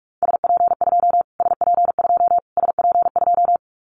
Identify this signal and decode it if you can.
Morse code, heard as a keyed tone, about 44 wpm, 700 Hz. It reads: SP2 SP2 SP2
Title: Callsign Prefixes for Top 25 DXCC - Repeated 3x - 45wpm